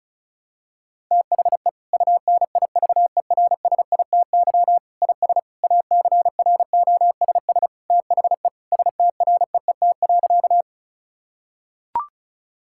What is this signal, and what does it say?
Morse code, heard as a keyed tone, about 36 wpm, 700 Hz. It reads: THE UNIVERSITY IS ACROSS THE STREET.
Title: Sentences from Top 1000 Words - Review Part 5 - 35wpm